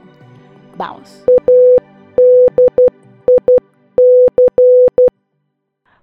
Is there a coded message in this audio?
Yes: Morse code from 1.3 to 5.1 s, heard as a keyed tone, about 12 wpm, 500 Hz, with speech. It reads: ADIC